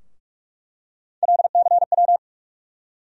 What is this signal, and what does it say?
Morse code, heard as a keyed tone, about 41 wpm, 700 Hz. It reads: LCW
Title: USA Call Sign Postfixes - 3 Letters 45wpm